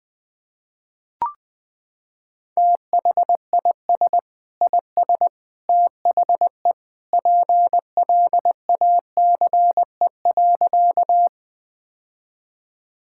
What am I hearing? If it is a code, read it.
Morse code, heard as a keyed tone, about 21 wpm, 700 Hz. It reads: THIS IS THE PLACE.
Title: Sentences with 3-Word Phrases - 20wpm